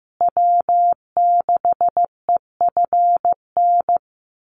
Morse code, heard as a keyed tone, about 15 wpm, 700 Hz. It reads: W6EFN
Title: USA Call Signs Repeated 3x 15wpm